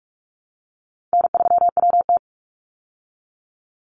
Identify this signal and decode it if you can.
Morse code, heard as a keyed tone, about 44 wpm, 700 Hz. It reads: N3WT
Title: Easy Call Signs Repeated 2x - 45wpm